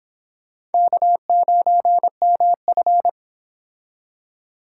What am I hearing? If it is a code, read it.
Morse code, heard as a keyed tone, about 30 wpm, 700 Hz. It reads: K9MF